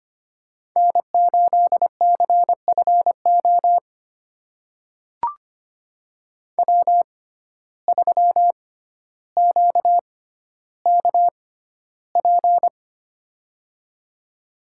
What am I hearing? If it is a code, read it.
Morse code, heard as a keyed tone, about 25 wpm, spaced as for 13 wpm, 700 Hz. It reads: N8CFO W3QKP
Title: USA Call Signs - 5 Characters - 6x Character Spacing - 25wpm